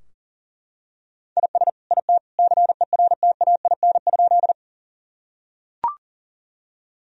Morse code, heard as a keyed tone, about 40 wpm, 700 Hz. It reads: IS IT CERTAIN?
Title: Sentences from Top 1000 Words - Review Part 2 - 40wpm